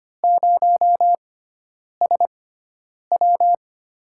Morse code, heard as a keyed tone, about 25 wpm, spaced as for 9 wpm, 700 Hz. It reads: 0SW
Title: USA Call Signs - 5 Characters - 6x Character Spacing - 25wpm